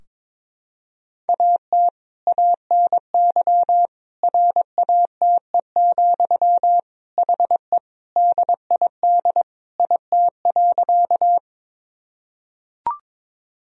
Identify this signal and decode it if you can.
Morse code, heard as a keyed tone, about 22 wpm, 700 Hz. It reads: AT ANY RATE, HE DID IT.
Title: Sentences from Top 1000 Words - Review Part 4 - 22wpm